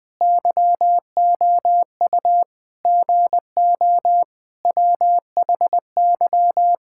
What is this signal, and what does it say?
Morse code, heard as a keyed tone, about 21 wpm, 700 Hz. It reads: YOU GO WHY
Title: Sets of 9 Words - 20wpm